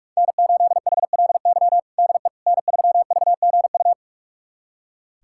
Morse code, heard as a keyed tone, about 46 wpm, 700 Hz. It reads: N8HLY DE N3VGU
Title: USA Callsign DE Callsign - 5 Characters - 45wpm